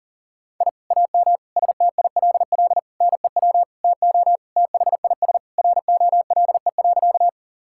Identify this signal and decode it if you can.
Morse code, heard as a keyed tone, about 40 wpm, 700 Hz. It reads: I AM STILL NEW TO THIS ROLE.
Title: Sentences from Top 1000 Words - Review Part 6 - 40wpm